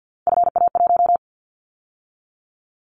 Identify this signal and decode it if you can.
Morse code, heard as a keyed tone, about 50 wpm, 700 Hz. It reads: FA1
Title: Callsign Prefixes for Top 50 DXCC - 50wpm